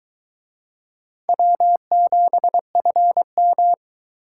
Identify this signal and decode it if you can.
Morse code, heard as a keyed tone, about 23 wpm, 700 Hz. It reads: W7FM